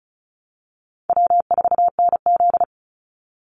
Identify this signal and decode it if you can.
Morse code, heard as a keyed tone, about 34 wpm, 700 Hz. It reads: W4NZ
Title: Easy Call Signs Repeated 2x - 35wpm